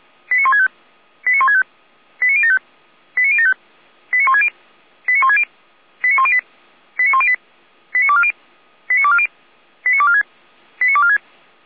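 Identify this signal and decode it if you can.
Five-tone selective call (ZVEI), heard as quick runs of five tones, tones of about 80 ms. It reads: 89165 89165 89075 89075 89160 89160 89108 89108 89240 89240 89246 89246
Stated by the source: ZVEI five-tone selective call 89167, received on SDR by Feuerwehr Orlamünde.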